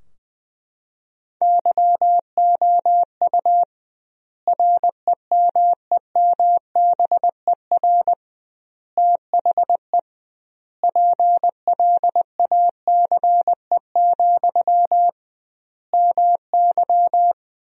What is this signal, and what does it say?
Morse code, heard as a keyed tone, about 20 wpm, 700 Hz. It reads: YOU REMEMBER THE PLACE, MY
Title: 2x Word-Spacing - Sentences from Top 400 Words - 20wpm